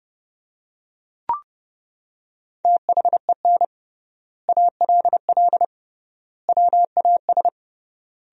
Morse code, heard as a keyed tone, about 32 wpm, 700 Hz. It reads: THEN ALL WAS
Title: 3x Word-Spacing - Sentences from Top 500 Words - 30wpm